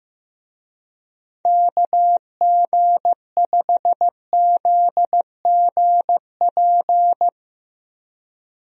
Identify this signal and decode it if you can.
Morse code, heard as a keyed tone, about 15 wpm, 700 Hz. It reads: KG5ZGP
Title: USA Call Signs - 6 Characters - Repeated 3x - 15wpm